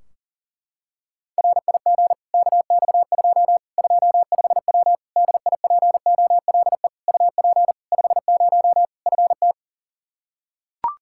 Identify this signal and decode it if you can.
Morse code, heard as a keyed tone, about 40 wpm, 700 Hz. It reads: RIG KX2 25W DIPOLE UP 50 FT
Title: K1USN SST QSO Practice 40wpm